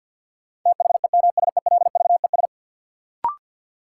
Morse code, heard as a keyed tone, about 53 wpm, 700 Hz. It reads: THEMSELVES
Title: Sentences from Top 800 Words - New Word Drill - 50wpm